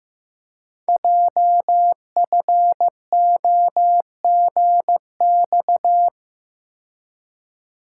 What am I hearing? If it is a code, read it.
Morse code, heard as a keyed tone, about 15 wpm, 700 Hz. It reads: JFOGX
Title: Code Groups - 5 Letters - 15wpm